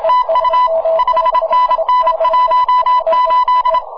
Morse code, heard as a keyed tone, about 27 wpm, 1000 Hz. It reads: TU 5NN 18
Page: На 15 метров, в лучшем виде - прекрасный сигнал!